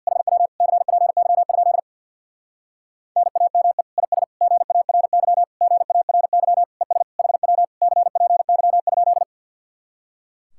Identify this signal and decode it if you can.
Morse code, heard as a keyed tone, about 58 wpm, 700 Hz. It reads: HW CPY? NAME IS GARY GARY ES HW CPY?
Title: Speed-Racing Common QSO Phrases - 50wpm